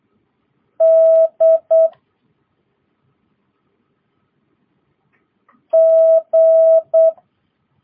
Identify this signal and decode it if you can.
Morse code, heard as a keyed tone, about 8 wpm, 650 Hz. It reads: D G
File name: jh8cbh / 2022cw